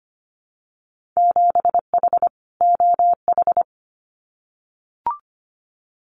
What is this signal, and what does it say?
Morse code, heard as a keyed tone, about 25 wpm, 700 Hz. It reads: 7H OH
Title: Winter Field Day Exchanges Encore 25wpm